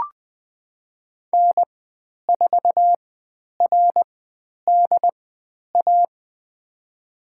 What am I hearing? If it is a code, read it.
Morse code, heard as a keyed tone, about 20 wpm, spaced as for 10 wpm, 700 Hz. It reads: N4RDA
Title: 109 Lesson 18g - Letter m - Call Signs 20wpm